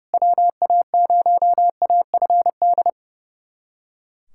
Morse code, heard as a keyed tone, about 32 wpm, 700 Hz. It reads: WA0AFD
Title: USA Callsign DE Callsign 30wpm